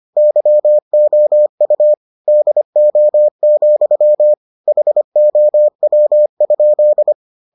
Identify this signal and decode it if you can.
Morse code, heard as a keyed tone, about 25 wpm, 600 Hz. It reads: YOU DO, HOW?